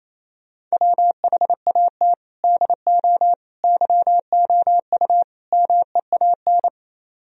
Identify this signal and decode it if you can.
Morse code, heard as a keyed tone, about 33 wpm, 700 Hz. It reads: WHAT DO YOU MEAN
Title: Speed-Racing - Sets of 4 Words 28wpm